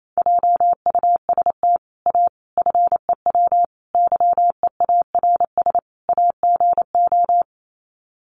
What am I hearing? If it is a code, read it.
Morse code, heard as a keyed tone, about 28 wpm, 700 Hz. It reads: JUST A FEW YEARS AGO
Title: Sets of 5 Words Encore 28wpm